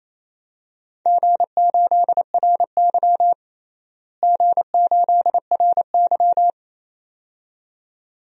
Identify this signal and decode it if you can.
Morse code, heard as a keyed tone, about 28 wpm, 700 Hz. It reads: G8RY G8RY
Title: DXCC-100 Call Signs - 4 Characters - Repeated 2x - 28wpm